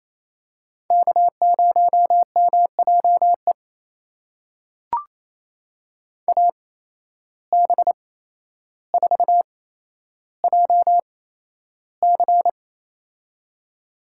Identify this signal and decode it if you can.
Morse code, heard as a keyed tone, about 28 wpm, spaced as for 12 wpm, 700 Hz. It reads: K0MJE AB4JC
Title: USA Call Signs - 5 Characters - 8x Character Spacing - 28wpm